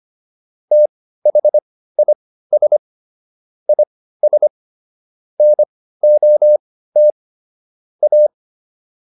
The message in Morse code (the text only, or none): THIS IS NOT A